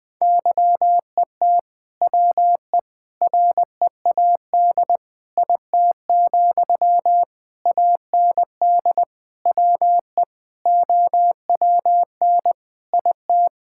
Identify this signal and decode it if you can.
Morse code, heard as a keyed tone, about 20 wpm, 700 Hz. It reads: YET WE READ IT, AND WE OWN IT
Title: Sentences from Top 1600 Words 20wpm